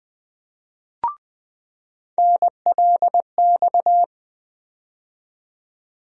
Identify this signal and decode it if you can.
Morse code, heard as a keyed tone, about 20 wpm, 700 Hz. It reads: NLX
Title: USA Call Sign Postfixes - 3 Letters 20wpm